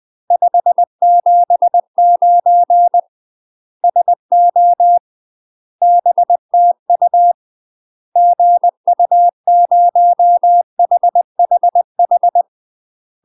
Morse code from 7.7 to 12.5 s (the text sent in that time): GU0HHH